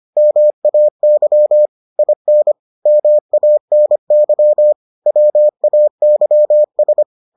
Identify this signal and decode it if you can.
Morse code, heard as a keyed tone, about 25 wpm, 600 Hz. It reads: MAY IN MANY WAYS